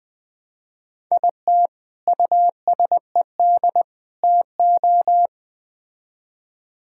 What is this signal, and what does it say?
Morse code, heard as a keyed tone, about 20 wpm, 700 Hz. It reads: IT USED TO
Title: Build A Phrase - 4-Word Phrases - 20wpm